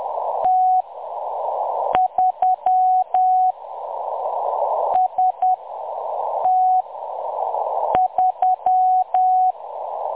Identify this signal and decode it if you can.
Morse code, heard as a keyed tone, about 10 wpm, spaced as for 6 wpm, 740 Hz. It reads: T3 ST3
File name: M23-ST3- -0858utc-10184khz